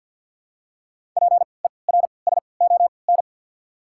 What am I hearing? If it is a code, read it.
Morse code, heard as a keyed tone, about 53 wpm, spaced as for 29 wpm, 700 Hz. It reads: PERSON
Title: 3x Character Spacing - Top 500 Words - 50wpm